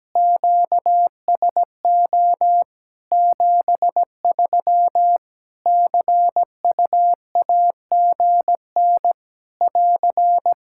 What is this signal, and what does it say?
Morse code, heard as a keyed tone, about 17 wpm, 700 Hz. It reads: QSO 73 CUAGN <AR>
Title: K1USN SST QSO Practice 17wpm